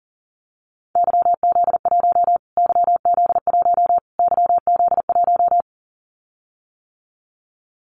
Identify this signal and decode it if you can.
Morse code, heard as a keyed tone, about 40 wpm, 700 Hz. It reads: YZ1 YZ1 YZ1
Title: Callsign Prefixes for Top 25 DXCC - Repeated 3x - 40wpm